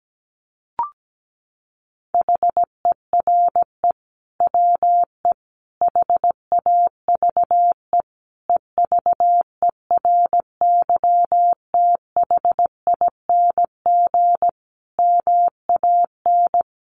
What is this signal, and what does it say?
Morse code, heard as a keyed tone, about 17 wpm, 700 Hz. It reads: HERE WE HAVE EVERYTHING MAN
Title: Sentences from Top 1100 Words 17wpm